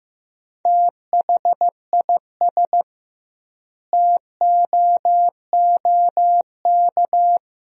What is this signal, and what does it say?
Morse code, heard as a keyed tone, about 15 wpm, 700 Hz. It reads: THIS TOOK
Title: 2x Word-Spacing - Sentences from Top 300 Words - 15wpm